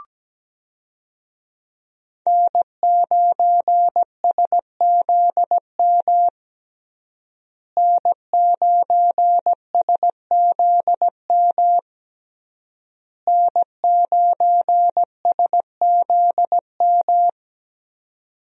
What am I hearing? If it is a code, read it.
Morse code, heard as a keyed tone, about 17 wpm, 700 Hz. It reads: N9SZM N9SZM N9SZM
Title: USA Call Signs - 5 Characters - Repeated 3x - 17wpm